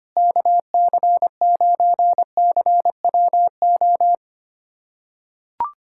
Morse code, heard as a keyed tone, about 25 wpm, 700 Hz. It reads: KC9CWO
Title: USA Callsign DE Callsign - 6 Characters - 25wpm